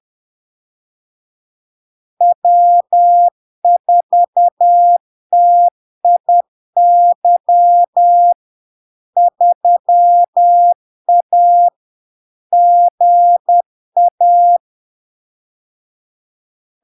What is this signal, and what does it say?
Morse code, heard as a keyed tone, about 10 wpm, 700 Hz. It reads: W4TIY 3A GA
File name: LARK-FD-2022-Log-in-CW-10wpm-700-Hz